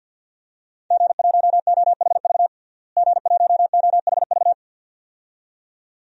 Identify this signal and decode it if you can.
Morse code, heard as a keyed tone, about 49 wpm, 700 Hz. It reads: G1OHV G1OHV
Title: DXCC-100 Call Signs - 5 Characters - Repeated 2x - 50wpm